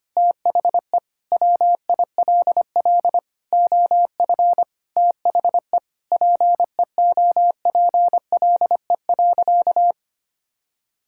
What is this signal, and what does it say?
Morse code, heard as a keyed tone, about 25 wpm, 700 Hz. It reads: THE WILL OF THE PEOPLE.
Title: Sentences from Top 1000 Words - Review Part 5 - 25wpm